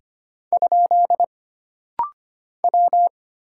Morse code, heard as a keyed tone, about 25 wpm, 700 Hz. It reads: ? W
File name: Single Character (Letter:Number:Punctuation:Prosign) - Rapid-Fire - 25wpm